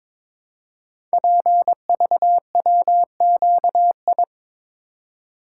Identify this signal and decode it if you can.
Morse code, heard as a keyed tone, about 22 wpm, 700 Hz. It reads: PVWQI
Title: Code Groups - 5 Letters - 22wpm